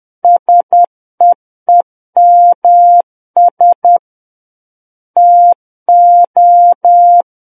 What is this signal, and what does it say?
Morse code, heard as a keyed tone, about 10 wpm, 700 Hz. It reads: SEEMS TO
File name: Book One / 01-Chapter 1 - 10 WPM